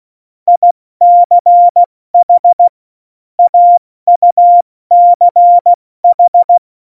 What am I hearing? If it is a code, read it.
Morse code, heard as a keyed tone, about 16 wpm, 700 Hz. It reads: ICH AUCH
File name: Morsekurs / Witze / Witz 20. August 2019, 70 F80